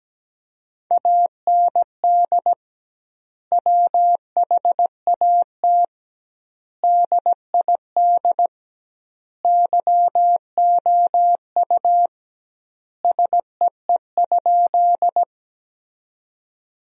Morse code, heard as a keyed tone, about 17 wpm, 700 Hz. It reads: AND WHAT DID YOU SEE?
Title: 2x Word-Spacing - Sentences from Top 100 Words - 17wpm